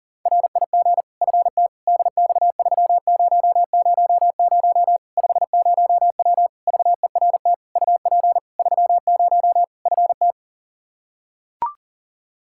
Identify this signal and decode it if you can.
Morse code, heard as a keyed tone, about 40 wpm, 700 Hz. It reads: RIG FT DX3000 50W VERT UP 30 FT
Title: Practice QSO - Exchange 2 - Info - 40wpm